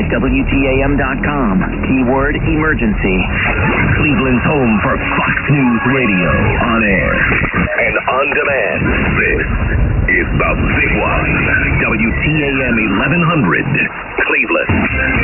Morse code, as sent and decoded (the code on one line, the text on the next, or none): none